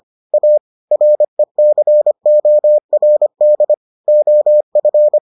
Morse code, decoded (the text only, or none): A RECORD OF